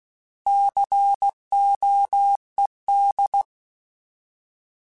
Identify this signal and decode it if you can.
Morse code, heard as a keyed tone, about 16 wpm, 800 Hz. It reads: COED
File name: Alphabetical / Alphabetical mp3s / Cs @15